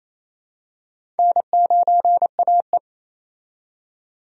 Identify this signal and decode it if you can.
Morse code, heard as a keyed tone, about 28 wpm, 700 Hz. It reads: N9AE
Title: USA Call Signs - 4 Characters - 28wpm